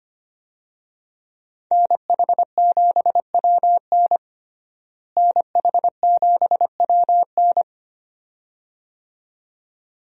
Morse code, heard as a keyed tone, about 25 wpm, 700 Hz. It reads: NH7WN NH7WN
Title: USA Call Signs - 5 Characters - Repeated 2x - 25wpm